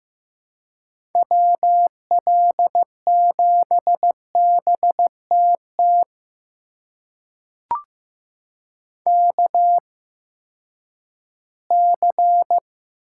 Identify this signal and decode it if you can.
Morse code, heard as a keyed tone, about 15 wpm, spaced as for 10 wpm, 700 Hz. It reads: WL7BTT KC
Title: USA Call Signs - 6 Characters - 8x Character Spacing - 15wpm